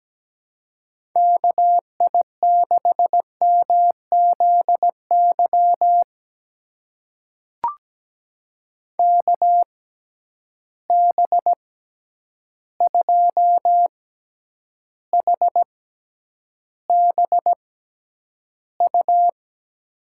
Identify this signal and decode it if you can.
Morse code, heard as a keyed tone, about 17 wpm, spaced as for 9 wpm, 700 Hz. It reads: KI6MZY KB2HBU
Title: USA Call Signs - 6 Characters - 6x Character Spacing - 17wpm